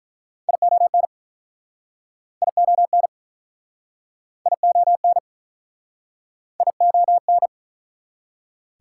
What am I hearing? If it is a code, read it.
Morse code, heard as a keyed tone, about 47 wpm, 700 Hz. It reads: ION ION ION ION